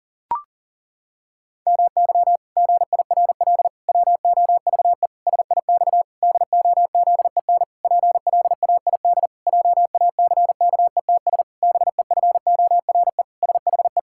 Morse code, heard as a keyed tone, about 40 wpm, 700 Hz. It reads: MY GIRL WOVE SIX DOZEN PLAID JACKETS BEFORE SHE
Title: Pangram Challenge v2 - 40wpm